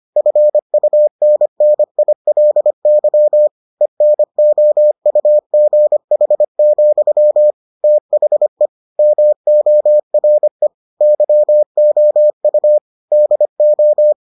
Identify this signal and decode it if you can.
Morse code, heard as a keyed tone, about 25 wpm, 600 Hz. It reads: FUNNILY ENOUGH, THE MORE YOU DO